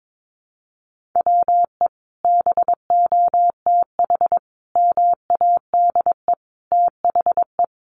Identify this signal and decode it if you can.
Morse code, heard as a keyed tone, about 22 wpm, 700 Hz. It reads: WE BOTH MADE THE
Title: Sentences from Top 1500 Words 22wpm